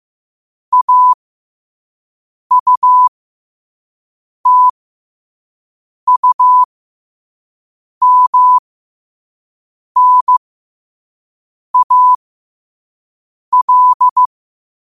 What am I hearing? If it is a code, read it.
Morse code, heard as a keyed tone, about 15 wpm, spaced as for 5 wpm, 1000 Hz. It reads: AUTUMNAL
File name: qotdfiles / 5wpm